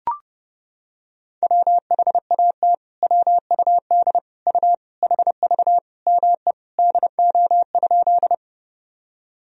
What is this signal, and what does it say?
Morse code, heard as a keyed tone, about 30 wpm, 700 Hz. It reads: WHAT WUD U HV ME DO?
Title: Sentences from Top 200 Words and Abbreviations 30wpm